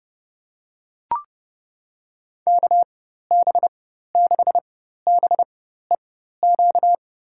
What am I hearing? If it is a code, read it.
Morse code, heard as a keyed tone, about 30 wpm, spaced as for 14 wpm, 700 Hz. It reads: KB6BEQ
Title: USA Call Signs - 6 Characters - 4x Character Spacing - 30wpm